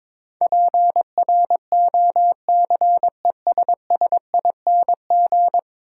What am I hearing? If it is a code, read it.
Morse code, heard as a keyed tone, about 22 wpm, 700 Hz. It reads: PROCESSING
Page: morse-code-ninja-practice-22wpm